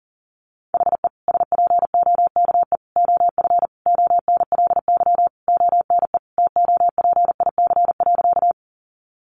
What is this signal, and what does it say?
Morse code, heard as a keyed tone, about 40 wpm, 700 Hz. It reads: HE SPOKE OF ONLY ONE TOPIC.
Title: Sentences from Top 1200 Words 40wpm